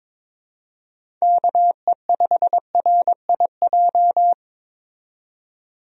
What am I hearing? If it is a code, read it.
Morse code, heard as a keyed tone, about 22 wpm, 700 Hz. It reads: KE5RIJ
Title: USA Call Signs - 6 Characters - 22wpm